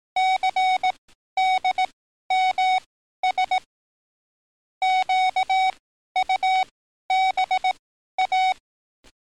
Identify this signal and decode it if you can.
Morse code, heard as a keyed tone, about 18 wpm, spaced as for 12 wpm, 750 Hz. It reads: CDMS QUBA
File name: training / NAVPERS 11049to11060 / NAVPERS 11053 RA RB